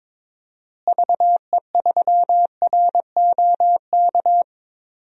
Morse code, heard as a keyed tone, about 22 wpm, 700 Hz. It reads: VE3ROK
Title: DXCC-100 Call Signs - 6 Characters - Repeated 2x - 22wpm